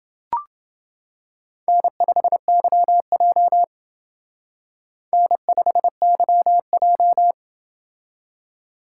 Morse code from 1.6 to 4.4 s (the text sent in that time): N5YJ